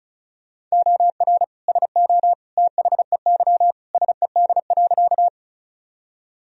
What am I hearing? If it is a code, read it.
Morse code, heard as a keyed tone, about 36 wpm, 700 Hz. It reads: OR SO THEY SED.
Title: Sentences from Top 100 Words and Abbreviations 35wpm